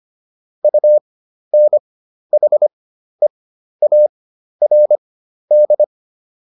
Morse code, heard as a keyed tone, about 25 wpm, spaced as for 12 wpm, 600 Hz. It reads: UNHEARD